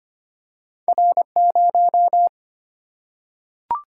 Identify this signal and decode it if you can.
Morse code, heard as a keyed tone, about 25 wpm, 700 Hz. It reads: R0
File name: Two Letter-Number Combinations - 25wpm